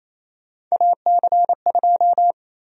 Morse code, heard as a keyed tone, about 28 wpm, 700 Hz. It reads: AC2
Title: USA Call Sign Prefixes - Repeated 2x - 28wpm